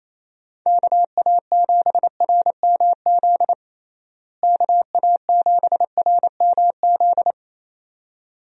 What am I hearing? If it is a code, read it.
Morse code, heard as a keyed tone, about 28 wpm, 700 Hz. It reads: KA7RMZ KA7RMZ
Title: USA Call Signs - 6 Characters - Repeated 2x - 28wpm